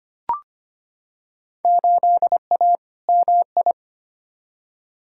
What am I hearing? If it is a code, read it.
Morse code, heard as a keyed tone, about 25 wpm, 700 Hz. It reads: 8A MI
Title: Field Day Exchanges Encore - 25wpm